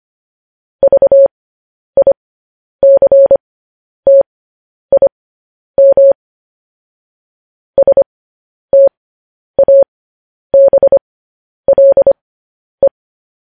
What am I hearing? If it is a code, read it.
Morse code, heard as a keyed tone, about 25 wpm, spaced as for 10 wpm, 550 Hz. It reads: VICTIM STABLE
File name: podcast / News headlines-1x-25x10